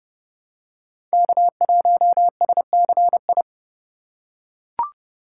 Morse code, heard as a keyed tone, about 29 wpm, 700 Hz. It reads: K1SCI